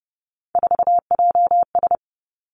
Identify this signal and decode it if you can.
Morse code, heard as a keyed tone, about 30 wpm, 700 Hz. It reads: 4JS
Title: Three Challenging Letter-Number Combinations 30wpm